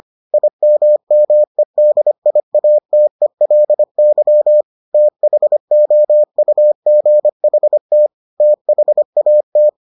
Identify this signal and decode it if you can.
Morse code, heard as a keyed tone, about 25 wpm, 600 Hz. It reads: IMMEDIATELY THOUGHT THAT